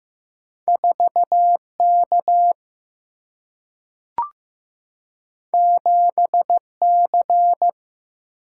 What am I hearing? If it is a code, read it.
Morse code, heard as a keyed tone, about 15 wpm, 700 Hz. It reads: 4K 7C
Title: Single-Letter Single-Number 15wpm